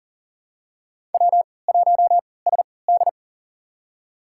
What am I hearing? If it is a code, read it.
Morse code, heard as a keyed tone, about 40 wpm, spaced as for 23 wpm, 700 Hz. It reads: W1SD
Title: USA Call Signs - 4 Characters - 3x Character Spacing - 40wpm